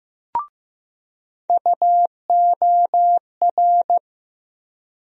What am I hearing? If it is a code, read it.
Morse code, heard as a keyed tone, about 15 wpm, 700 Hz. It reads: UOR
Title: USA Call Sign Postfixes - 3 Letters 15wpm